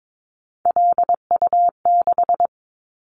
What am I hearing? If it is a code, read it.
Morse code, heard as a keyed tone, about 22 wpm, 700 Hz. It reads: LU6
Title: Callsign Prefixes for Top 100 DXCC - 22wpm